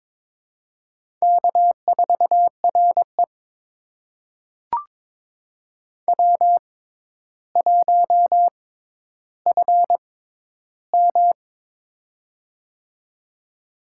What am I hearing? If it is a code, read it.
Morse code, heard as a keyed tone, about 21 wpm, spaced as for 11 wpm, 700 Hz. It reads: K4RE W1FM
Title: USA Call Signs - 4 Characters - 6x Character Spacing - 22wpm